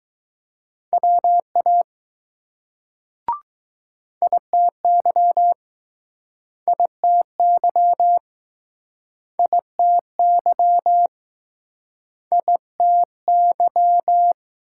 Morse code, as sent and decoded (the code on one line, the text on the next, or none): .-- .- / .. - -.-- / .. - -.-- / .. - -.-- / .. - -.--
WA ITY ITY ITY ITY